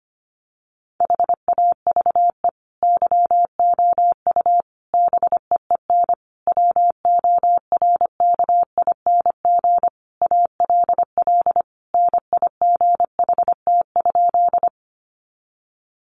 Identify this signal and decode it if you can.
Morse code, heard as a keyed tone, about 25 wpm, 700 Hz. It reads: HAVE YOU BEEN WORKING ALL NIGHT?
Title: Sentences from Top 700 Words 25wpm